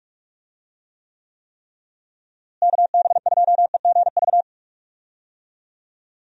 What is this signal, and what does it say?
Morse code, heard as a keyed tone, about 43 wpm, 700 Hz. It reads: KB2EGV
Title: USA Call Signs - 6 Characters - Repeated 3x - 45wpm